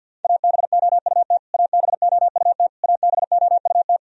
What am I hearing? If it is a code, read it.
Morse code, heard as a keyed tone, about 52 wpm, 700 Hz. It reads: ABOUT ABOUT ABOUT
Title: Top 100 Words Repeated 50wpm